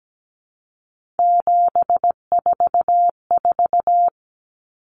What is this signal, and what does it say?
Morse code, heard as a keyed tone, about 17 wpm, 700 Hz. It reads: 744
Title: Numbers - 3-Digit 17wpm